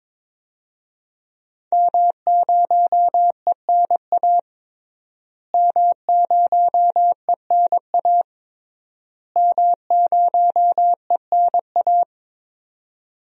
Morse code, heard as a keyed tone, about 21 wpm, 700 Hz. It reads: M0ENA M0ENA M0ENA
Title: DXCC-100 Call Signs - 5 Characters - Repeated 3x - 22wpm